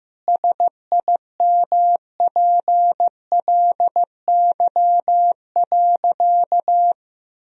Morse code, heard as a keyed tone, about 15 wpm, 700 Hz. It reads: SIMPLY.
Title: Sentences from Top 1000 Words - Review Part 4 - 15wpm